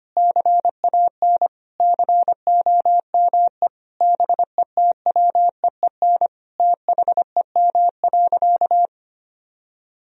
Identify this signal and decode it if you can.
Morse code, heard as a keyed tone, about 25 wpm, 700 Hz. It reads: CAN COME BETWEEN THEM.
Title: Sentences from Top 1000 Words - Review Part 1 - 25wpm